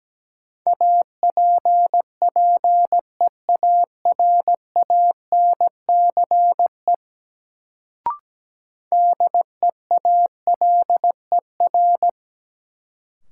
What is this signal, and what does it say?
Morse code, heard as a keyed tone, about 17 wpm, 700 Hz. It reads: APPEARANCE DEALER
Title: Sentences from Top 1100 Words - New Word Drill - 17wpm